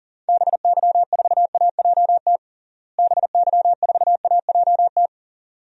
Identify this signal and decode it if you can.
Morse code, heard as a keyed tone, about 40 wpm, 700 Hz. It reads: BY4AJT BY4AJT
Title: DXCC-100 Call Signs - 6 Characters - Repeated 2x - 40wpm